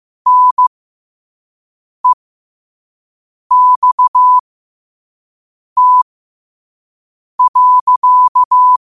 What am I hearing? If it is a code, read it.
Morse code, heard as a keyed tone, about 15 wpm, spaced as for 5 wpm, 1000 Hz. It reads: NEXT.
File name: qotdfiles / 5wpm